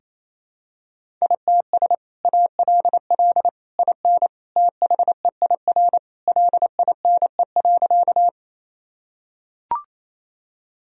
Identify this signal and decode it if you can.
Morse code, heard as a keyed tone, about 28 wpm, 700 Hz. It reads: ITS ALL IN THEIR LINE.
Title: Sentences from Top 1000 Words - Review Part 1 - 28wpm